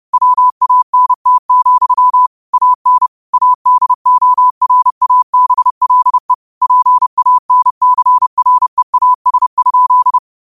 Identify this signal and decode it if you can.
Morse code, heard as a keyed tone, about 30 wpm, 1000 Hz. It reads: WANT, AN ADORABLE PANCREAS?
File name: qotdfiles / 30wpm